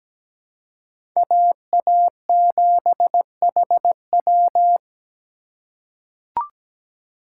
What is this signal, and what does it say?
Morse code, heard as a keyed tone, about 17 wpm, 700 Hz. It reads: AA7HW
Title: USA Call Signs - 5 Characters - 6x Character Spacing - 17wpm